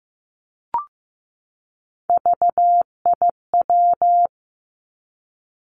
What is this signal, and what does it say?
Morse code, heard as a keyed tone, about 15 wpm, 700 Hz. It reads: VIW
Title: Three Challenging Letter-Number Combinations 15wpm